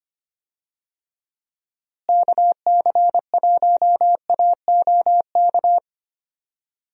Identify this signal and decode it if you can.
Morse code, heard as a keyed tone, about 25 wpm, 700 Hz. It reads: KC1AOK
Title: USA Call Signs - 6 Characters - Repeated 2x - 25wpm